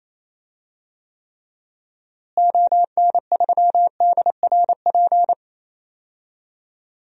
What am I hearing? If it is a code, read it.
Morse code, heard as a keyed tone, about 28 wpm, 700 Hz. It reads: ON3DRP
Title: DXCC-100 Call Signs - 6 Characters - Repeated 2x - 28wpm